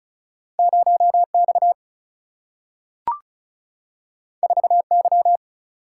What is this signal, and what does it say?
Morse code, heard as a keyed tone, about 36 wpm, 700 Hz. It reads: 0 X 4 Y
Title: Single-Letter Single-Number 35wpm